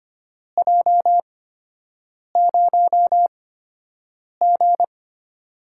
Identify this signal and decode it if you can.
Morse code, heard as a keyed tone, about 25 wpm, spaced as for 7 wpm, 700 Hz. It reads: J0G
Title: USA Call Signs - 4 Characters - 8x Character Spacing - 25wpm